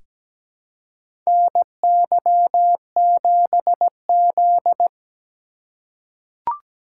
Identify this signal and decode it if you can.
Morse code, heard as a keyed tone, about 17 wpm, 700 Hz. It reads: NY7Z
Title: K1USN SST QSO Practice - 17wpm